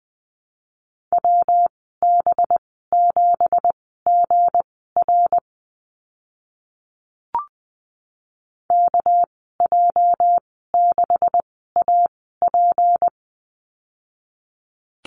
Morse code, heard as a keyed tone, about 20 wpm, spaced as for 15 wpm, 700 Hz. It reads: WB7GR KJ6AP
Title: 245 Lesson 39g - Letter z - Call Signs 20wpm